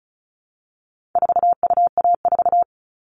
Morse code, heard as a keyed tone, about 34 wpm, 700 Hz. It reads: 4UA4
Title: All Cut Numbers - 4-Digit 35wpm